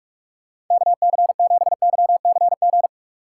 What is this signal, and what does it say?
Morse code, heard as a keyed tone, about 37 wpm, 700 Hz. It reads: KC7YCG
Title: Speed Racing - USA 2x3 Call Signs 30wpm